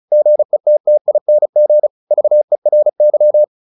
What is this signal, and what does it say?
Morse code, heard as a keyed tone, about 36 wpm, 600 Hz. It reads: GETTING VERY
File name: the old man and the sea / the old man and the sea 1 35wpm